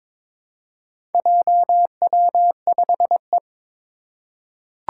Morse code, heard as a keyed tone, about 22 wpm, 700 Hz. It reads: JW5E
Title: Four Challenging Letter-Number Combinations 22wpm